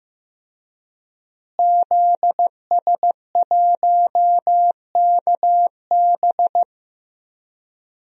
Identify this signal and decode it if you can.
Morse code, heard as a keyed tone, about 15 wpm, 700 Hz. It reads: ZS1KB
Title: DXCC-100 Call Signs - 5 Characters - Repeated 2x - 15wpm